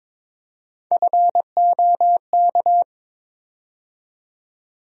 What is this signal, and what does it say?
Morse code, heard as a keyed tone, about 22 wpm, 700 Hz. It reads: FOK
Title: USA Call Sign Postfixes - 3 Letters 22wpm